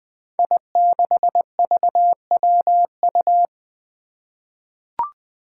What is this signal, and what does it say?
Morse code, heard as a keyed tone, about 20 wpm, 700 Hz. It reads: I6VWU
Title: Five Challenging Letter-Number Combinations 20wpm